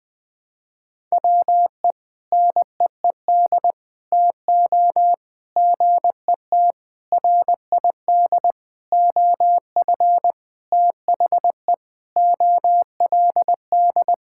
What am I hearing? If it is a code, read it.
Morse code, heard as a keyed tone, about 20 wpm, 700 Hz. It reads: WE NEED TO GET RID OF THE OLD
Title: Sentences with 4-Word Phrases - 20wpm